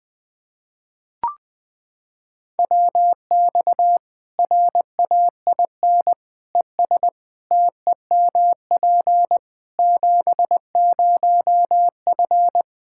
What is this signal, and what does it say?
Morse code, heard as a keyed tone, about 20 wpm, 700 Hz. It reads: WX RAIN ES TEMP 70F
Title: Practice QSO - Exchange 2 - Info - 20wpm